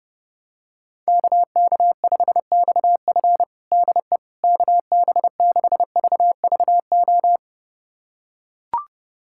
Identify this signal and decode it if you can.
Morse code, heard as a keyed tone, about 32 wpm, 700 Hz. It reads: KK5XF DE KB6VVO
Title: USA Callsign DE Callsign 30wpm